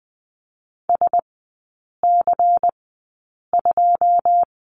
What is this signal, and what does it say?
Morse code, heard as a keyed tone, about 20 wpm, spaced as for 8 wpm, 700 Hz. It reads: SC2
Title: 100 Lesson 17e - Letter u - Three Characters 20wpm